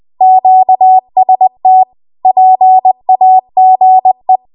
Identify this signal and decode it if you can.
Morse code, heard as a keyed tone, about 20 wpm, 750 Hz. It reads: QST PAGE
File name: Archive / 20 WPM / 20WPM